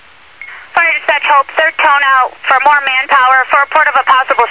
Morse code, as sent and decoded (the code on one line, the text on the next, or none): none